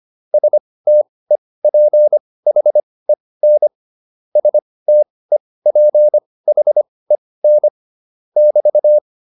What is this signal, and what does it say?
Morse code, heard as a keyed tone, about 25 wpm, spaced as for 18 wpm, 600 Hz. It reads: STEPHEN STEPHEN <BT>